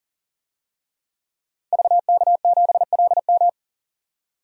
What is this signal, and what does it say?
Morse code, heard as a keyed tone, about 40 wpm, 700 Hz. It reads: VK7LM
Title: DXCC-100 Call Signs - 5 Characters - 40wpm